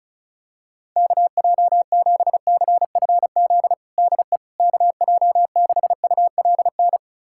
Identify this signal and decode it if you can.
Morse code, heard as a keyed tone, about 34 wpm, 700 Hz. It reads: KJ7CFZ DE KJ6ULN
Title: USA Callsign DE Callsign - 6 Characters - 35wpm